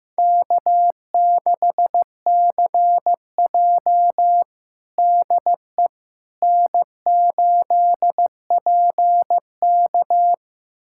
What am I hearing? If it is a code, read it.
Morse code, heard as a keyed tone, about 15 wpm, 700 Hz. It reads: K6CJ DE N8PK
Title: USA Callsign DE Callsign - 4 Characters - 15wpm